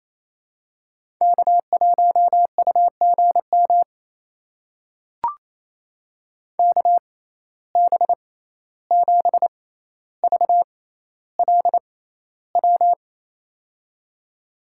Morse code, heard as a keyed tone, about 28 wpm, spaced as for 14 wpm, 700 Hz. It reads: K1UGM KB7VLW
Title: USA Call Signs - 6x Character Spacing - 28wpm